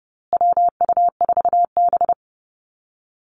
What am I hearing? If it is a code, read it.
Morse code, heard as a keyed tone, about 30 wpm, 700 Hz. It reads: WU4B
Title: Call Signs Repeated 2x - 30wpm